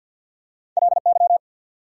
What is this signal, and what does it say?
Morse code, heard as a keyed tone, about 54 wpm, 690 Hz. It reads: LY
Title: Prefixes and Suffixes 50wpm